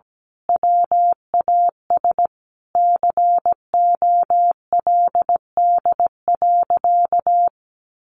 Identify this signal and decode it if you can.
Morse code, heard as a keyed tone, about 17 wpm, 700 Hz. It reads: WAS COLD.
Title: A Sound Of Thunder 17wpm